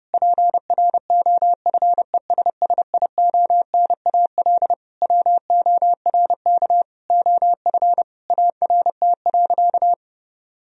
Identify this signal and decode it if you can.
Morse code, heard as a keyed tone, about 30 wpm, 700 Hz. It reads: PROFESSIONAL WORK OF ART.
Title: Sentences from Top 1000 Words - Review Part 4 - 30wpm